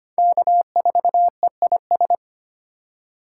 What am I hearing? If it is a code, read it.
Morse code, heard as a keyed tone, about 32 wpm, 700 Hz. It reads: K4EIS